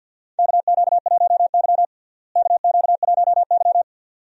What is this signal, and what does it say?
Morse code, heard as a keyed tone, about 47 wpm, 700 Hz. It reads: KQ1Y KQ1Y
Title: USA Call Signs - 4 Characters - Repeated 2x - 50wpm